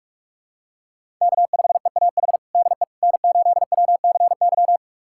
Morse code, heard as a keyed tone, about 43 wpm, 700 Hz. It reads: K5EAH DE N8WCY
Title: USA Callsign DE Callsign - 5 Characters - 45wpm